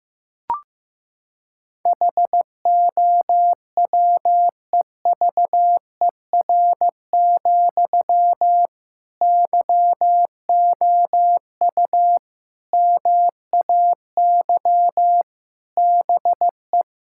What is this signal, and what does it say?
Morse code, heard as a keyed tone, about 15 wpm, 700 Hz. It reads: HOWEVER, YOU MAY BE
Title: Sentences from Top 1600 Words 15wpm